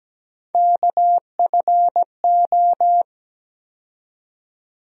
Morse code, heard as a keyed tone, about 17 wpm, 700 Hz. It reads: KFO
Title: USA Call Sign Postfixes - 3 Letters 17wpm